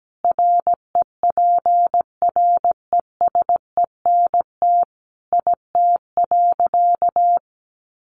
Morse code, heard as a keyed tone, about 17 wpm, 700 Hz. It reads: REPRESENT IT.
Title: Sentences from Top 1200 Words 17wpm